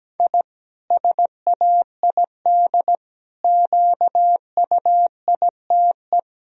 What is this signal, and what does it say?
Morse code, heard as a keyed tone, about 17 wpm, 700 Hz. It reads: I SAID QUITE
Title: Sentences from Top 1000 Words - Review Part 4 - 17wpm